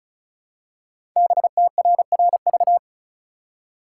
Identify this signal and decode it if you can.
Morse code, heard as a keyed tone, about 36 wpm, 700 Hz. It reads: BTRRV
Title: Code Groups - 5 Letters - 35wpm